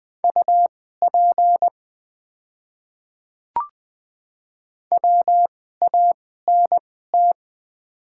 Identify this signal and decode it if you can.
Morse code, heard as a keyed tone, about 20 wpm, spaced as for 14 wpm, 700 Hz. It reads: UP WANT
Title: 2x Character Spacing - Top 200 Words - 20wpm